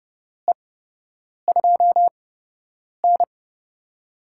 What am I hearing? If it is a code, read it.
Morse code, heard as a keyed tone, about 30 wpm, spaced as for 8 wpm, 700 Hz. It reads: E2N
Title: USA Call Signs - 8x Character Spacing - 30wpm